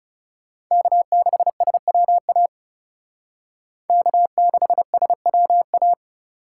Morse code, heard as a keyed tone, about 36 wpm, 700 Hz. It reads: K6SWA K6SWA
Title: Speed Racing - USA 1x3 Call Signs 30wpm